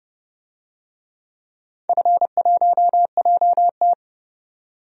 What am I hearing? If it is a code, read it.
Morse code, heard as a keyed tone, about 29 wpm, 700 Hz. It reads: F1JT
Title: DXCC-100 Call Signs - 4 Characters - 30wpm